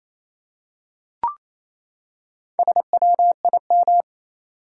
Morse code, heard as a keyed tone, about 28 wpm, 700 Hz. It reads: SWIM
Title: Four Letter Words - 28wpm